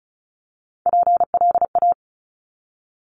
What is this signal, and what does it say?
Morse code, heard as a keyed tone, about 36 wpm, 700 Hz. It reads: PLA
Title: Top Three Letter Combinations in Words 35wpm